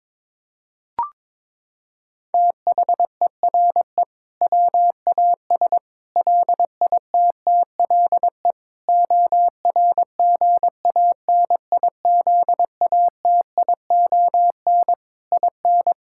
Morse code, heard as a keyed tone, about 22 wpm, 700 Hz. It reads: THERE WAS LITTLE ORGANIZATION IN
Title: Sentences from Top 1000 Words - Review Part 5 - 22wpm